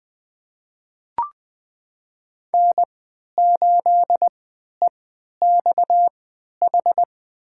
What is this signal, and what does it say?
Morse code, heard as a keyed tone, about 20 wpm, spaced as for 11 wpm, 700 Hz. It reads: N8EXH
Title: USA Call Signs - 5 Characters - 3x Character Spacing - 20wpm